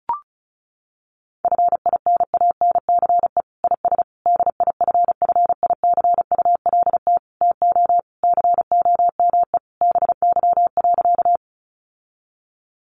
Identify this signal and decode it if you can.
Morse code, heard as a keyed tone, about 34 wpm, 700 Hz. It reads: FINANCE IS DIFFICULT TO COME BY.
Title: Sentences from Top 1500 Words 35wpm